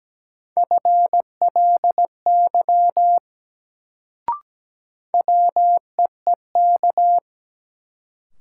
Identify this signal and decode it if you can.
Morse code, heard as a keyed tone, about 17 wpm, 700 Hz. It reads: FLY WEEK
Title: Sentences from Top 500 Words - New Word Drill - 17wpm